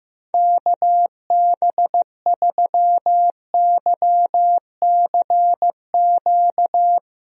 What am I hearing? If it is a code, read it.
Morse code, heard as a keyed tone, about 16 wpm, 700 Hz. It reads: KB3YCQ
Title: USA Call Signs - 6 Characters - Repeated 3x - 15wpm